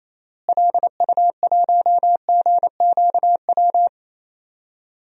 Morse code, heard as a keyed tone, about 28 wpm, 700 Hz. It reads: LU1GQW
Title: DXCC-100 Call Signs - 6 Characters - Repeated 2x - 28wpm